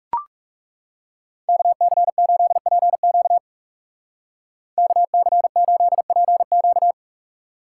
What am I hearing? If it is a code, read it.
Morse code, heard as a keyed tone, about 35 wpm, 700 Hz. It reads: KC8PQ KC8PQ
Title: Speed Racing - USA Call Signs 30wpm